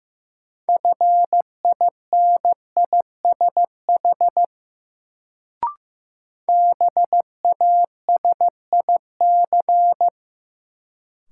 Morse code, heard as a keyed tone, about 15 wpm, 700 Hz. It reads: FINISH BASIC
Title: Sentences from Top 1000 Words - Review Part 3 - New Word Drill - 15wpm